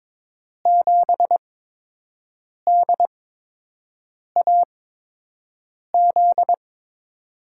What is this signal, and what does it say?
Morse code, heard as a keyed tone, about 21 wpm, spaced as for 6 wpm, 700 Hz. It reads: 7DAZ
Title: USA Call Signs - 5 Characters - 8x Character Spacing - 22wpm